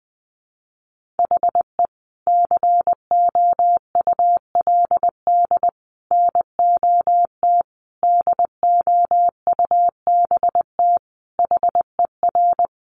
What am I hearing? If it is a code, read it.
Morse code, heard as a keyed tone, about 20 wpm, 700 Hz. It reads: HE COULD NOT DOUBT HER
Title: Sentences from Top 1500 Words 20wpm